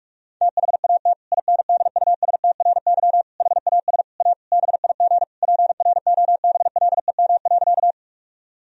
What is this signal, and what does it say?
Morse code, heard as a keyed tone, about 48 wpm, 700 Hz. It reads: THAT INDUSTRY HAS A BIG PROBLEM.
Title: Sentences from Top 1000 Words - Review Part 4 - 45wpm